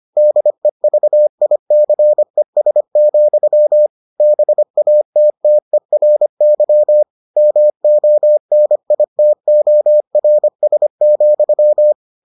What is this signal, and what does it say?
Morse code, heard as a keyed tone, about 25 wpm, 600 Hz. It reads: DEVICES, BATTERY MONITORS,